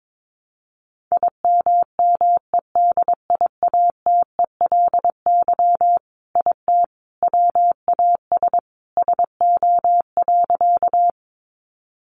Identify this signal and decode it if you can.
Morse code, heard as a keyed tone, about 22 wpm, 700 Hz. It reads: IMMEDIATELY IT WAS SO.
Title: Sentences from Top 1100 Words 22wpm